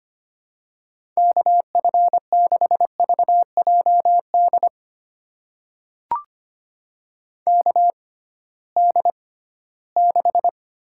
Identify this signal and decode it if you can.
Morse code, heard as a keyed tone, about 25 wpm, spaced as for 16 wpm, 700 Hz. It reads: KF6VJD KD6
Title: USA Call Signs - 6 Characters - 6x Character Spacing - 25wpm